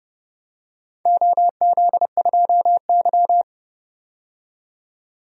Morse code, heard as a keyed tone, about 30 wpm, 700 Hz. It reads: OZ2Y
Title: DXCC-100 Call Signs - 4 Characters - 30wpm